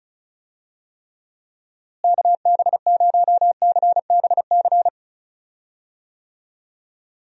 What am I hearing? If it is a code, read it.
Morse code, heard as a keyed tone, about 34 wpm, 700 Hz. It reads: KB0CBC
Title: USA Call Signs - 6 Characters - Repeated 2x - 35wpm